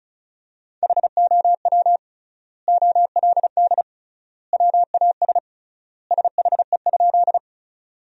Morse code, heard as a keyed tone, about 36 wpm, 700 Hz. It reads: HOW OLD WAS SHE?
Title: 3x Word-Spacing - Sentences from Top 200 Words - 35wpm